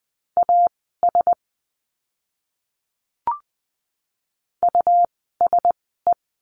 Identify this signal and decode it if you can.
Morse code, heard as a keyed tone, about 20 wpm, spaced as for 14 wpm, 700 Hz. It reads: AS USE
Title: 192 Lesson 31f - Letter v - Words 20wpm